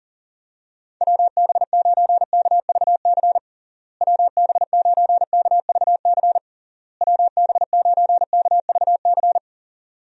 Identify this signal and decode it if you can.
Morse code, heard as a keyed tone, about 40 wpm, 700 Hz. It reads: WB9KVC WB9KVC WB9KVC
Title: USA Call Signs - 6 Characters - Repeated 3x - 40wpm